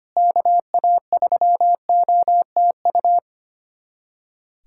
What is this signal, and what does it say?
Morse code, heard as a keyed tone, about 25 wpm, 700 Hz. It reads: KA3OTU
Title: USA Callsign DE Callsign 25wpm